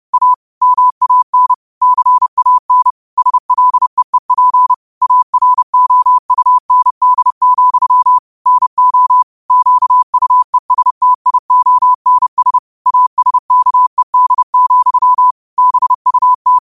Morse code, heard as a keyed tone, about 30 wpm, 1000 Hz. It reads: A MAN CAN SLEEP AROUND, NO QUESTIONS ASKED, BUT
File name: qotdfiles / 30wpm